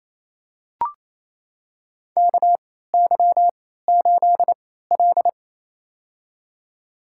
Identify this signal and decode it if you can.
Morse code, heard as a keyed tone, about 28 wpm, spaced as for 16 wpm, 700 Hz. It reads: KY8L
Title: USA Call Signs - 4 Characters - 3x Character Spacing - 28wpm